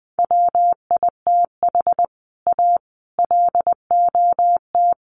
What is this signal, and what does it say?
Morse code, heard as a keyed tone, about 20 wpm, 700 Hz. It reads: WITH A LOT
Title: Sets of 4 Words 20wpm